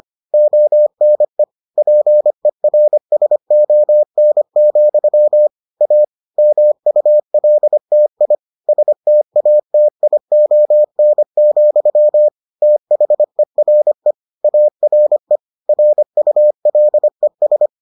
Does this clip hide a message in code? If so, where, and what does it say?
Yes: Morse code from 0.3 to 17.7 s, heard as a keyed tone, about 25 wpm, 600 Hz. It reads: ONE PERSON, A MULTI STATION, THERE ARE RULES